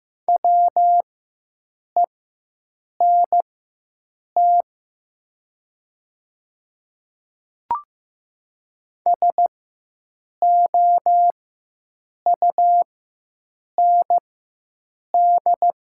Morse code, heard as a keyed tone, about 15 wpm, spaced as for 7 wpm, 700 Hz. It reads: WENT SOUND
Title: 4x Character Spacing - Top 300 Words - 15wpm